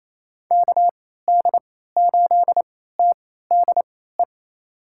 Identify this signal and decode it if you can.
Morse code, heard as a keyed tone, about 28 wpm, spaced as for 16 wpm, 700 Hz. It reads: KD8TDE
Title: USA Call Signs - 6 Characters - 3x Character Spacing - 28wpm